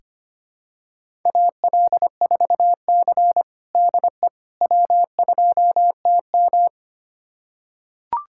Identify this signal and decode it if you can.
Morse code, heard as a keyed tone, about 25 wpm, 700 Hz. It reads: AL4C DE W2TM
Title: USA Callsign DE Callsign - 4 Characters - 25wpm